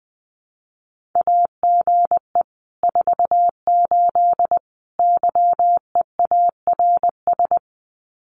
Morse code, heard as a keyed tone, about 20 wpm, 700 Hz. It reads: AGE 48 YEARS
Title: Common QSO Phrases Encore 20wpm